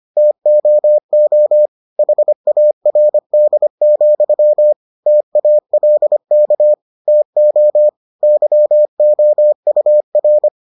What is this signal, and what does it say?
Morse code, heard as a keyed tone, about 25 wpm, 600 Hz. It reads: TOO HARD, TALK TO YOUR